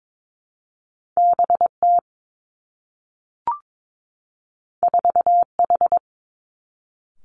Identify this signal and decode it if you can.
Morse code, heard as a keyed tone, about 22 wpm, 700 Hz. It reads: B T 4 H
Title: Two Challenging Letter-Number Combinations 22wpm